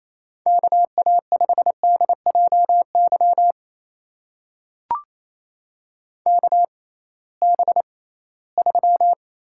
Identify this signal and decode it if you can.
Morse code, heard as a keyed tone, about 28 wpm, spaced as for 18 wpm, 700 Hz. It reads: KA5DJY KB3
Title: USA Call Signs - 6 Characters - 6x Character Spacing - 28wpm